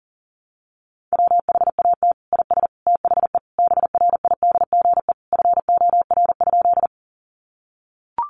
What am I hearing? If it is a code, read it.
Morse code, heard as a keyed tone, about 40 wpm, 700 Hz. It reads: WHAT IS THE BRIDGE FOR?
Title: Sentences from Top 1200 Words 40wpm